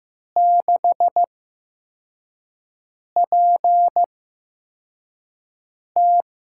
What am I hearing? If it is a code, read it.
Morse code, heard as a keyed tone, about 15 wpm, spaced as for 4 wpm, 700 Hz. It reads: 6PT
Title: USA Call Signs - 5 Characters - 8x Character Spacing - 15wpm